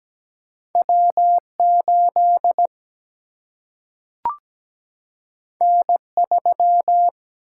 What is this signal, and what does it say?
Morse code, heard as a keyed tone, about 17 wpm, 700 Hz. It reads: W8 N3
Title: Single-Letter Single-Number 17wpm